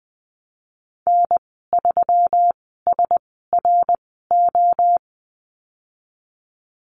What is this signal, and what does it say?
Morse code, heard as a keyed tone, about 20 wpm, spaced as for 15 wpm, 700 Hz. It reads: N3SRO
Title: 140 Lesson 23g - Letter f - Call Signs 20wpm